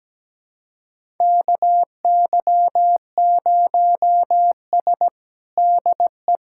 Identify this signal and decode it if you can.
Morse code, heard as a keyed tone, about 17 wpm, 700 Hz. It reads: KY0S DE
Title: USA Callsign DE Callsign - 4 Characters - 17wpm